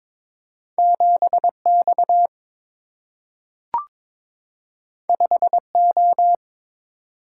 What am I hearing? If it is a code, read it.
Morse code, heard as a keyed tone, about 22 wpm, 700 Hz. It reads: 7X 5O
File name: practice / 22wpm / Two Letter-Number Combinations - 22wpm